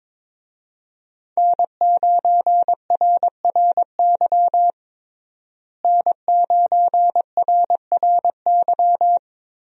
Morse code, heard as a keyed tone, about 22 wpm, 700 Hz. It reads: N9RRY N9RRY
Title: USA Call Signs - 5 Characters - Repeated 2x - 22wpm